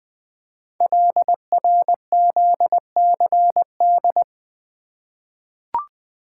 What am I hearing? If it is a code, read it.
Morse code, heard as a keyed tone, about 20 wpm, 700 Hz. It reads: LRZCD
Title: Code Groups - 5 Letters - 20wpm